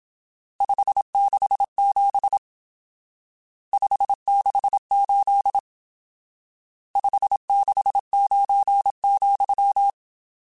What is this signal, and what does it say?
Morse code, heard as a keyed tone, about 27 wpm, 800 Hz. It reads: 567 568 569,